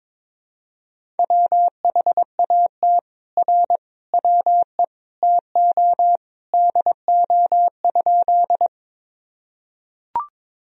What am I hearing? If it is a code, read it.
Morse code, heard as a keyed tone, about 21 wpm, 700 Hz. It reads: WHAT R WE TO DO?
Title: Sentences from Top 100 Words and Abbreviations 22wpm